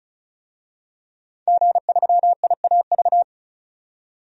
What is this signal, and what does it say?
Morse code, heard as a keyed tone, about 34 wpm, 700 Hz. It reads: G3IAV
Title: DXCC-100 Call Signs - 5 Characters - 35wpm